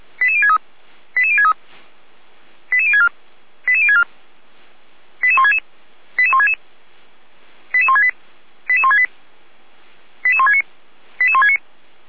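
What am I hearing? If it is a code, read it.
Five-tone selective call (ZVEI), heard as quick runs of five tones, tones of about 70 ms. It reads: 88063 88063 88064 88064 88166 88166 88168 88168 88169 88169
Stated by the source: ZVEI five-tone selective call received via SDR.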